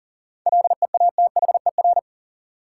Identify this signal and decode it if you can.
Morse code, heard as a keyed tone, about 40 wpm, 700 Hz. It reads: LEATHER
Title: Seven Letter Words - 40wpm